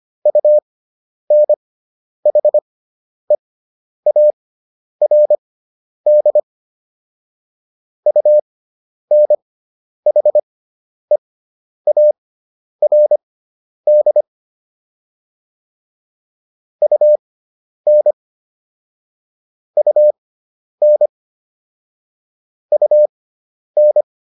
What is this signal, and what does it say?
Morse code, heard as a keyed tone, about 25 wpm, spaced as for 10 wpm, 600 Hz. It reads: UNHEARD UNHEARD UN UN UN